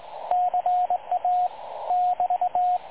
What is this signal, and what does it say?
Morse code, heard as a keyed tone, about 21 wpm, 720 Hz, with radio noise. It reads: CA <BT>
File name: dlazaawansowanych / nagraniazpasma / websdr recording 3565.0kHz